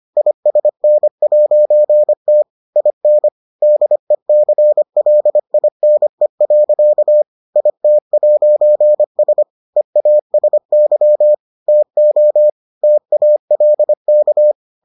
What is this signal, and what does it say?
Morse code, heard as a keyed tone, about 25 wpm, 600 Hz. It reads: ISN'T IN DECLINE. IT'S EASY TO TALK